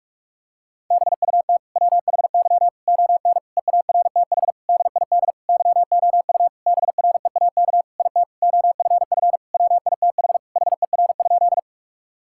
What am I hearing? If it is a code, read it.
Morse code, heard as a keyed tone, about 41 wpm, 700 Hz. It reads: BUT WHY ON EARTH DID YOU BREAK IT OFF WITH HER?
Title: Sentences from Top 1000 Words - Review Part 6 - 45wpm